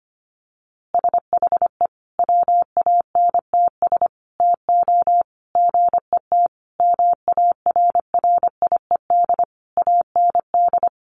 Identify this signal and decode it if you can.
Morse code, heard as a keyed tone, about 25 wpm, 700 Hz. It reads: SHE WANTS TO GET MARRIED AND
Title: Easy Phrases 25wpm